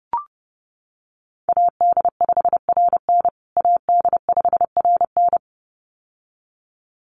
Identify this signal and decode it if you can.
Morse code, heard as a keyed tone, about 30 wpm, 700 Hz. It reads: AD5RN AD5RN
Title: USA Call Signs Repeated 2x 30wpm